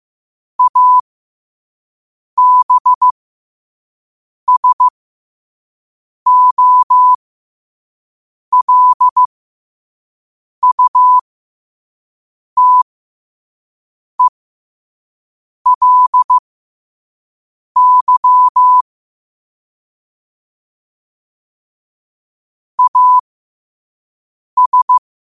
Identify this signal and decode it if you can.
Morse code, heard as a keyed tone, about 15 wpm, spaced as for 5 wpm, 1000 Hz. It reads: ABSOLUTELY AS
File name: qotdfiles / 5wpm